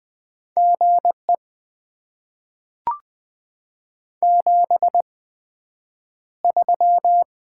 Morse code, heard as a keyed tone, about 20 wpm, spaced as for 9 wpm, 700 Hz. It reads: GE 73
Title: Top 20 QSO Elements - 8x Character Spacing - 20wpm